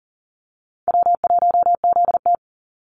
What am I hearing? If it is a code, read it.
Morse code, heard as a keyed tone, about 39 wpm, 700 Hz. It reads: W1ZT
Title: Call Signs Repeated 2x - 40wpm